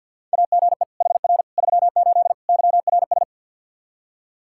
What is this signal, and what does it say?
Morse code, heard as a keyed tone, about 52 wpm, 700 Hz. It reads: AGE HR 38 YRS
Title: Practice QSO - Exchange 3 - More Info - 50wpm